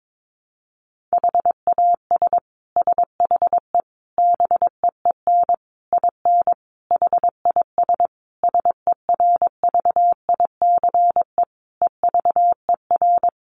Morse code, heard as a keyed tone, about 22 wpm, 700 Hz. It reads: HAS SHE BEEN IN HIS SERVICE EVER
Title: Sentences from Top 700 Words 22wpm